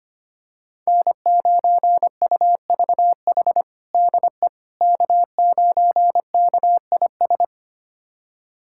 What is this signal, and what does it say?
Morse code, heard as a keyed tone, about 25 wpm, 700 Hz. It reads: N9UVH DE K9KIS
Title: USA Callsign DE Callsign - 5 Characters - 25wpm